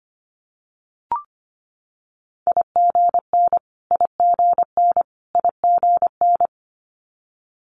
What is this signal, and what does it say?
Morse code, heard as a keyed tone, about 25 wpm, 700 Hz. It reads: IGN IGN IGN
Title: Multi-letter Phonemes Repeated 3x 25wpm